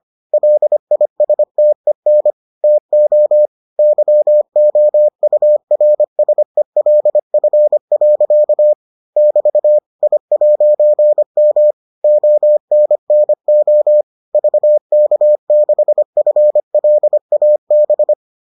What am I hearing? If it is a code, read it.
Morse code, heard as a keyed tone, about 25 wpm, 600 Hz. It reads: LISTEN TO YOURSELF. <BT> I'M ONNO VK6FLAB